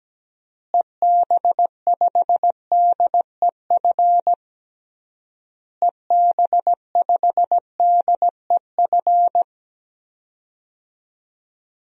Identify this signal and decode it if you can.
Morse code, heard as a keyed tone, about 17 wpm, 700 Hz. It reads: EB5DEF EB5DEF
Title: DXCC-100 Call Signs - 6 Characters - Repeated 3x - 17wpm